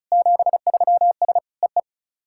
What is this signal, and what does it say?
Morse code, heard as a keyed tone, about 36 wpm, 690 Hz. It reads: 73S EE
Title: SOTA QSO Practice 35wpm